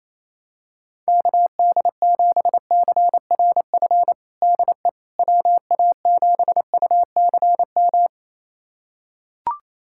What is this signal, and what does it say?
Morse code, heard as a keyed tone, about 28 wpm, 700 Hz. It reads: KD7CRF DE WA7UCM
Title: USA Callsign DE Callsign - 6 Characters - 28wpm